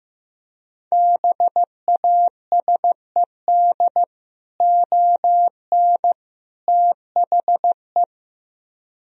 Morse code, heard as a keyed tone, about 15 wpm, 700 Hz. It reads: BASED ON THE
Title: Speed-Racing - Sets of 3 Words 15wpm